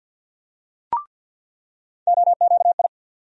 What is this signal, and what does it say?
Morse code, heard as a keyed tone, about 50 wpm, 700 Hz. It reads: OQI
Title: USA Call Sign Postfixes - 3 Letters 50wpm